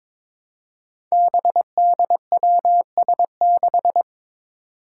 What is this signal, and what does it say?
Morse code, heard as a keyed tone, about 22 wpm, 700 Hz. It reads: BDWS6
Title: Five Challenging Letter-Number Combinations 22wpm